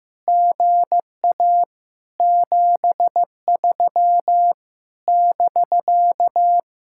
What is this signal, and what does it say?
Morse code, heard as a keyed tone, about 15 wpm, 700 Hz. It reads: GA 73 <BK>
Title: Speed-Racing Common QSO Phrases Encore - 15wpm